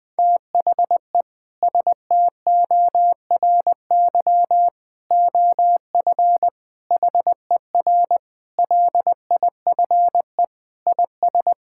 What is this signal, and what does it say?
Morse code, heard as a keyed tone, about 20 wpm, 700 Hz. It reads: THE STORY OF HER LIFE IS
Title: Sentences with 3-Word Phrases - Spoken Beginning - 20wpm